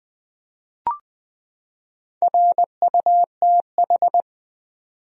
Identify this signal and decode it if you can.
Morse code, heard as a keyed tone, about 20 wpm, 700 Hz. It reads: RUTH
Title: USA Common Names - Max 4 Letters - 20wpm